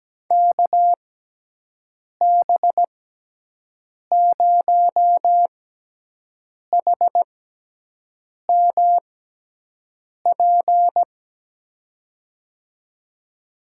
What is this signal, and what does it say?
Morse code, heard as a keyed tone, about 17 wpm, spaced as for 6 wpm, 700 Hz. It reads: KB0HMP
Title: USA Call Signs - 6x Character Spacing - 17wpm